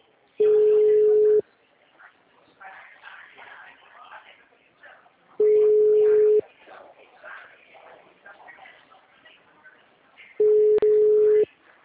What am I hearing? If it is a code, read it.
Ringback tone (Europe, 425 Hz), with speech.